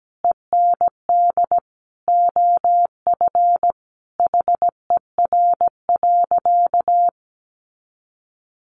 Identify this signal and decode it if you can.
Morse code, heard as a keyed tone, about 17 wpm, 700 Hz. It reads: END OF HER.
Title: Sentences from Top 200 Words 17wpm